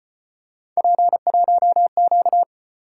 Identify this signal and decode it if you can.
Morse code, heard as a keyed tone, about 32 wpm, 700 Hz. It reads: P1Q